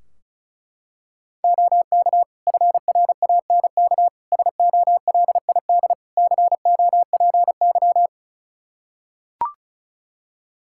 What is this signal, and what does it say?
Morse code, heard as a keyed tone, about 34 wpm, 700 Hz. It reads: OK FRANK SOLID COPY
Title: Practice QSO - Exchange 3 - More Info - 35wpm